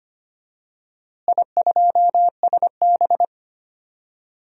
Morse code, heard as a keyed tone, about 25 wpm, 700 Hz. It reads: I2SB
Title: DXCC-100 Call Signs - 4 Characters - Repeated 3x - 25wpm